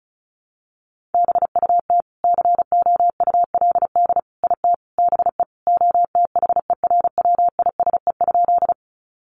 Morse code, heard as a keyed tone, about 34 wpm, 700 Hz. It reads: BUT COULD IT BE OTHERWISE?
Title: Sentences from Top 1200 Words 35wpm